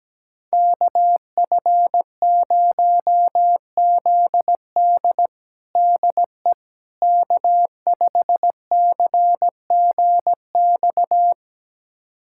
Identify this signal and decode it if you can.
Morse code, heard as a keyed tone, about 17 wpm, 700 Hz. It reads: KF0ZD DE K5CGX
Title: USA Callsign DE Callsign - 5 Characters - 17wpm